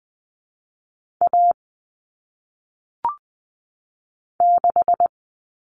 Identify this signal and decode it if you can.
Morse code, heard as a keyed tone, about 20 wpm, 700 Hz. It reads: A 6
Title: Common Cut Numbers - 1-Digit 20wpm